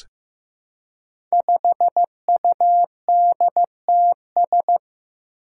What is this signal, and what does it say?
Morse code, heard as a keyed tone, about 15 wpm, 700 Hz. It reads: 5UDTS
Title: Five Challenging Letter-Number Combinations 15wpm